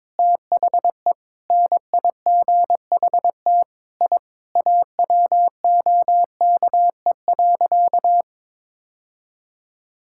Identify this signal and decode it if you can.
Morse code, heard as a keyed tone, about 21 wpm, 700 Hz. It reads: THE NIGHT I AWOKE.
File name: practice / 22wpm / Sentences with 4-Word Phrases - 22wpm